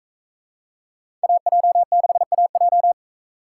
Morse code, heard as a keyed tone, about 42 wpm, 700 Hz. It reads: AJ6AJ